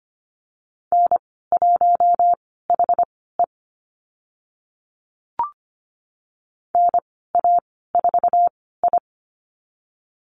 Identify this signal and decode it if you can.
Morse code, heard as a keyed tone, about 25 wpm, spaced as for 16 wpm, 690 Hz. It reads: N1HE NA4I
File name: Lesson 15g - Number 5 - Call Signs 25wpm